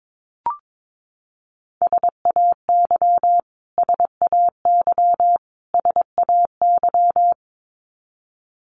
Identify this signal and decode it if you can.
Morse code, heard as a keyed tone, about 22 wpm, 700 Hz. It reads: SAY SAY SAY
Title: Top 500 Words Repeated 22wpm